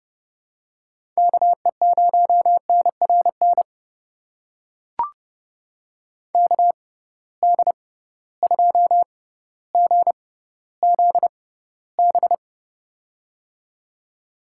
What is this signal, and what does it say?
Morse code, heard as a keyed tone, about 30 wpm, spaced as for 15 wpm, 700 Hz. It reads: KE0NRN KD2GZB
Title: USA Call Signs - 6x Character Spacing - 30wpm